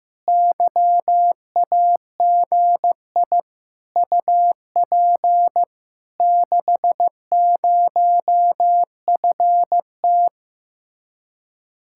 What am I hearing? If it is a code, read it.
Morse code, heard as a keyed tone, about 15 wpm, 700 Hz. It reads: YAGI UP 60FT
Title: Speed-Racing Common QSO Phrases Encore - 15wpm